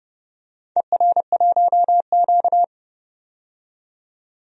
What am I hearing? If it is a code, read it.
Morse code, heard as a keyed tone, about 30 wpm, 700 Hz. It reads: ER1Q
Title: DXCC-100 Call Signs - 4 Characters - 30wpm